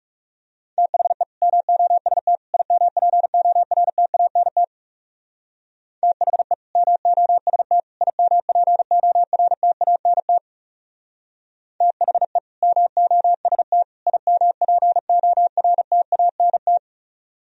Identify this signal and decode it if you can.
Morse code, heard as a keyed tone, about 39 wpm, 700 Hz. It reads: THE MOST IMPORTANT THE MOST IMPORTANT THE MOST IMPORTANT